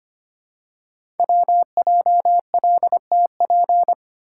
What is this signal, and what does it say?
Morse code, heard as a keyed tone, about 25 wpm, 700 Hz. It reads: WJLTP
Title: Code Groups - 5 Letters - 25wpm